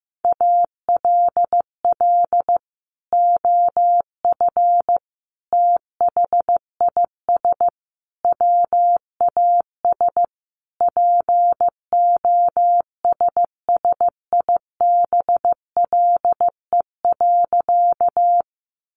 Morse code, heard as a keyed tone, about 15 wpm, 700 Hz. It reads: ALL OF THIS WAS POSSIBLE.
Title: Sentences from Top 500 Words 15wpm